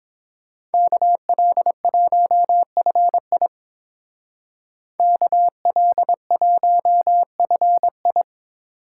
Morse code, heard as a keyed tone, about 27 wpm, 700 Hz. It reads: KL1FI KL1FI
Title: Speed Racing - USA 2x2 Call Signs 22wpm